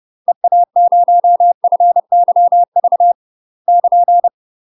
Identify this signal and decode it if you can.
Morse code, heard as a keyed tone, about 30 wpm, 700 Hz. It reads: EA0FYV <KN>
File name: UK MASTER QSOs TRAINING / 30WPM / UK MASTER QSOs 0045